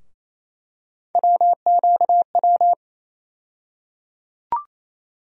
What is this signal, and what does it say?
Morse code, heard as a keyed tone, about 28 wpm, 700 Hz. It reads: WQW
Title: USA Call Sign Postfixes - 3 Letters 28wpm